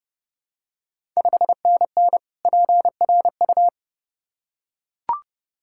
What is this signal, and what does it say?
Morse code, heard as a keyed tone, about 30 wpm, 700 Hz. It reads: 5NN PRU
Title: Ontario QSO Party Exchange 30wpm